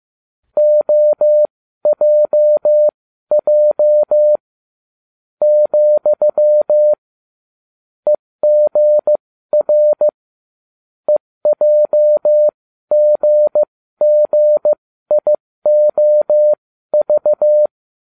Morse code, heard as a keyed tone, about 15 wpm, spaced as for 13 wpm, 600 Hz. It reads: OJJ , EGR EJGGIOV